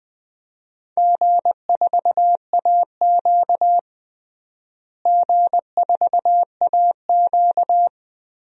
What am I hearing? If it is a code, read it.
Morse code, heard as a keyed tone, about 20 wpm, 700 Hz. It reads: G4AQ G4AQ
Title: DXCC-100 Call Signs - 4 Characters - Repeated 3x - 20wpm